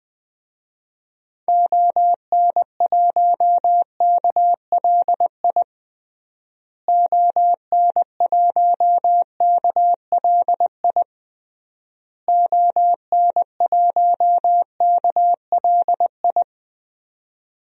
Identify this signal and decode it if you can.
Morse code, heard as a keyed tone, about 20 wpm, 700 Hz. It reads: ON1KLI ON1KLI ON1KLI
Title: DXCC-100 Call Signs - 6 Characters - Repeated 3x - 20wpm